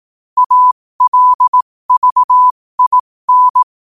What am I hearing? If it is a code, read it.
Morse code, heard as a keyed tone, about 18 wpm, spaced as for 15 wpm, 1000 Hz. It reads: ALVIN
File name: qotdfiles / 15wpm